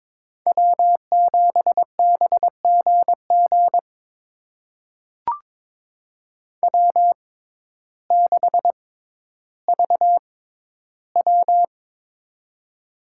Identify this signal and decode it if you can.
Morse code, heard as a keyed tone, about 22 wpm, spaced as for 12 wpm, 700 Hz. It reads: W7BGG W6VW
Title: USA Call Signs - 6x Character Spacing - 22wpm